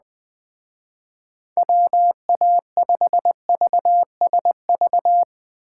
Morse code, heard as a keyed tone, about 20 wpm, 700 Hz. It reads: WA5VSV
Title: Speed Racing - USA Call Signs 17wpm